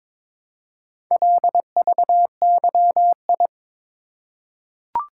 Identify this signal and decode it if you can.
Morse code, heard as a keyed tone, about 22 wpm, 700 Hz. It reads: LVYI
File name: practice / 22wpm / Four Letter-Number Combinations - 22wpm